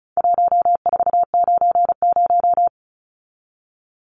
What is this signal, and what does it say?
Morse code, heard as a keyed tone, about 34 wpm, 700 Hz. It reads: 1490
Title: Numbers - 4-Digit 35wpm